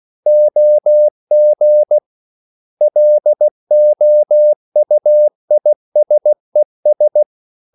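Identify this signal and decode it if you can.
Morse code, heard as a keyed tone, about 16 wpm, 600 Hz. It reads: OG LOUISES
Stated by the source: Slow (QRS) Morse code at 16 wpm, from CW med Gnister.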